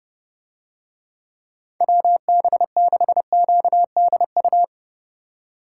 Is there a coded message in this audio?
Yes: Morse code from 1.8 to 4.6 s, heard as a keyed tone, about 31 wpm, 700 Hz. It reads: WB6QDU